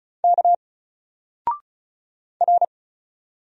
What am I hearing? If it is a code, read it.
Morse code, heard as a keyed tone, about 36 wpm, 700 Hz. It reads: K R
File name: Single-Letter-Number-35wpm